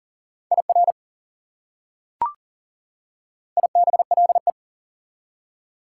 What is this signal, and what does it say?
Morse code, heard as a keyed tone, about 40 wpm, 700 Hz. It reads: IR IBLE
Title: Prefixes and Suffixes 40wpm